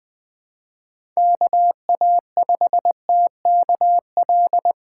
Morse code, heard as a keyed tone, about 20 wpm, 700 Hz. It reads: KA5TKL
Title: USA Call Signs - 6 Characters - 4x Character Spacing - 20wpm